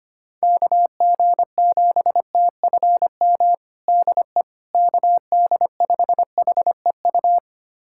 Morse code, heard as a keyed tone, about 25 wpm, 700 Hz. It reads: KG7TFM DE KD5HEU
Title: USA Callsign DE Callsign - 6 Characters - 25wpm